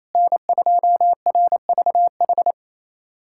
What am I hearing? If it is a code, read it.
Morse code, heard as a keyed tone, about 28 wpm, 700 Hz. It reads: N2RVH
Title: USA Callsign DE Callsign - 5 Characters - 28wpm